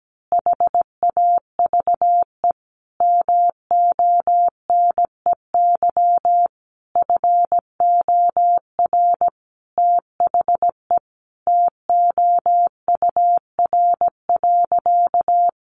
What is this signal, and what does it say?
Morse code, heard as a keyed tone, about 17 wpm, 700 Hz. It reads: HAVE MONEY FOR THE TOUR.
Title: Sentences from Top 1400 Words 17wpm